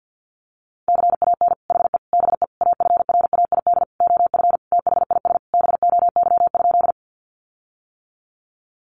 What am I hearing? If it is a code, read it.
Morse code, heard as a keyed tone, about 50 wpm, 700 Hz. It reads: CAN HE BE AFRAID OF THIS BOY?
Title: Sentences from Top 1500 Words 50wpm